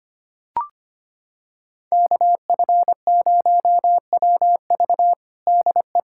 Morse code, heard as a keyed tone, about 25 wpm, 700 Hz. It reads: KF0WV DE
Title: USA Callsign DE Callsign 25wpm